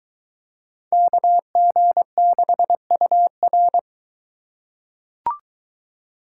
Morse code, heard as a keyed tone, about 23 wpm, 700 Hz. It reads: KG6UR